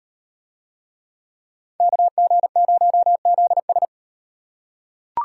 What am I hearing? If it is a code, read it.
Morse code, heard as a keyed tone, about 38 wpm, 700 Hz. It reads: KG0ZS